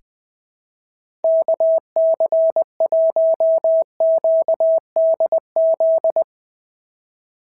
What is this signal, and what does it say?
Morse code, heard as a keyed tone, about 19 wpm, 650 Hz. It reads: KC1QDZ
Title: POTA QSO Practice - 20wpm